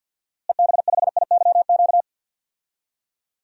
Morse code, heard as a keyed tone, about 48 wpm, 700 Hz. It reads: EB5IYQ
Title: DXCC-100 Call Signs - 6 Characters - Repeated 2x - 50wpm